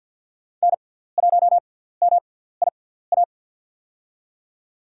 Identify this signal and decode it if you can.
Morse code, heard as a keyed tone, about 47 wpm, spaced as for 17 wpm, 700 Hz. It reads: N1MIA
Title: USA Call Signs - 5 Characters - 6x Character Spacing - 50wpm